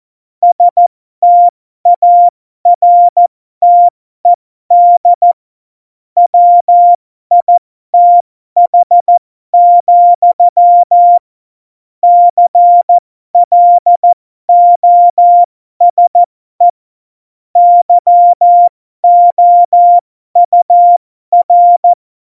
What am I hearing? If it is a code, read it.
Morse code, heard as a keyed tone, about 14 wpm, spaced as for 12 wpm, 700 Hz. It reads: STARTED WITH, CLOSE YOUR
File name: Morsekurs / Witze